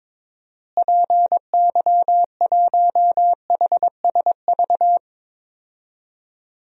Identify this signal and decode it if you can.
Morse code, heard as a keyed tone, about 22 wpm, 700 Hz. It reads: PY1HSV
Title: DXCC-100 Call Signs - 6 Characters - Repeated 2x - 22wpm